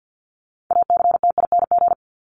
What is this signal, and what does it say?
Morse code, heard as a keyed tone, about 49 wpm, 700 Hz. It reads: ACTING
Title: ING Suffix - Top 100 Words with Base Words 50wpm